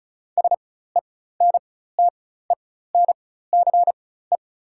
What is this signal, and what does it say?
Morse code, heard as a keyed tone, about 36 wpm, spaced as for 16 wpm, 700 Hz. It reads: SENTENCE
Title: 4x Character Spacing - Top 200 Words - 35wpm